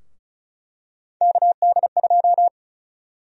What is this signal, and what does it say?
Morse code, heard as a keyed tone, about 36 wpm, 700 Hz. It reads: KD2
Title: USA Call Sign Prefixes - Repeated 2x - 35wpm